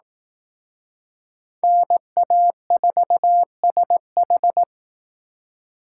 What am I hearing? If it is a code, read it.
Morse code, heard as a keyed tone, about 18 wpm, 700 Hz. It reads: NA4SH
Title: Speed Racing - USA 2x2 Call Signs 15wpm